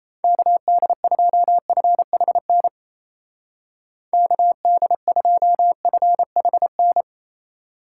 Morse code, heard as a keyed tone, about 36 wpm, 700 Hz. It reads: KD2FHN KD2FHN